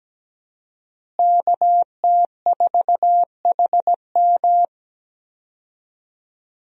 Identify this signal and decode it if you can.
Morse code, heard as a keyed tone, about 17 wpm, 700 Hz. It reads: KT4HM
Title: USA Call Signs - 5 Characters - Repeated 2x - 17wpm